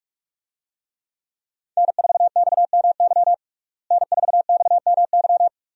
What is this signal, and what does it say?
Morse code, heard as a keyed tone, about 46 wpm, 700 Hz. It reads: N4XMY N4XMY
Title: USA Call Signs - 5 Characters - Repeated 3x - 45wpm